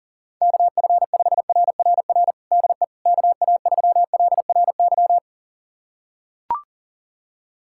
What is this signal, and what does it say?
Morse code, heard as a keyed tone, about 40 wpm, 700 Hz. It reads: KF5RRR DE KA3LRY
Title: USA Callsign DE Callsign - 6 Characters - 40wpm